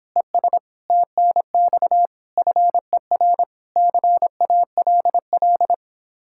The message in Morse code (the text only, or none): ES TNX FER CALL